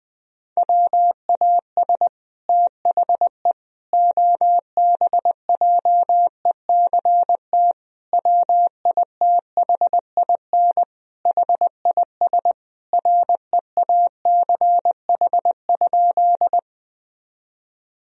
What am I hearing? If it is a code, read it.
Morse code, heard as a keyed tone, about 20 wpm, 700 Hz. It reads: WAS THE OBJECT WITHIN HIS REACH?
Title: Sentences from Top 1000 Words - Review Part 4 - 20wpm